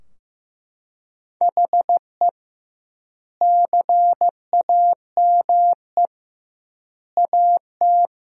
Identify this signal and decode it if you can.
Morse code, heard as a keyed tone, about 15 wpm, 700 Hz. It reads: HE CAME AT
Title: 2x Word-Spacing - Sentences from Top 300 Words - 15wpm